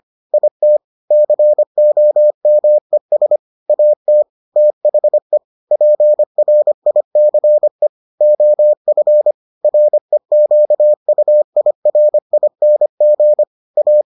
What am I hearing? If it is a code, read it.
Morse code, heard as a keyed tone, about 25 wpm, 600 Hz. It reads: IT COMES AT THE PRICE OF REQUIRING A